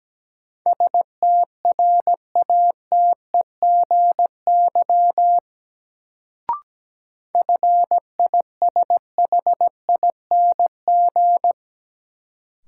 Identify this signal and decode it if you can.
Morse code, heard as a keyed tone, about 17 wpm, 700 Hz. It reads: STRATEGY FISHING
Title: Sentences from Top 1000 Words - Review Part 5 - New Word Drill - 17wpm